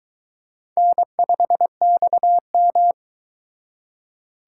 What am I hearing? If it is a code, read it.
Morse code, heard as a keyed tone, about 23 wpm, 700 Hz. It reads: N5XM